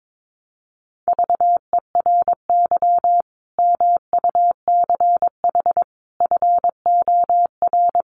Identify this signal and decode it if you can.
Morse code, heard as a keyed tone, about 22 wpm, 700 Hz. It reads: VERY MUCH FOR
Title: Sets of 5 Words 22wpm